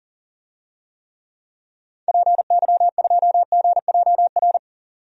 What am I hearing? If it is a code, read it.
Morse code, heard as a keyed tone, about 40 wpm, 700 Hz. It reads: PY2GJR
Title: DXCC-100 Call Signs - 6 Characters - 40wpm